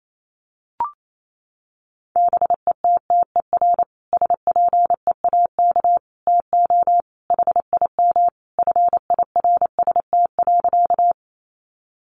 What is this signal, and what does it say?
Morse code, heard as a keyed tone, about 28 wpm, 700 Hz. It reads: BETTER SPEAK TO HIM FIRST.
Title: Sentences from Top 1100 Words 28wpm